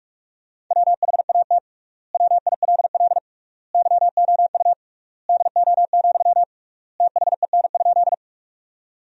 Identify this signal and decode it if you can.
Morse code, heard as a keyed tone, about 41 wpm, 690 Hz. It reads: WHAT WILL YOU DO, THEN?
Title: 3x Word-Spacing - Sentences from Top 100 Words - 45wpm